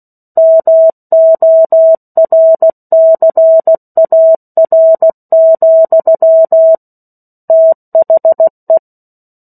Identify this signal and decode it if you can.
Morse code, heard as a keyed tone, about 16 wpm, 650 Hz. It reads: MORCAR, THE
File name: 03-Chapter-03 16 WPM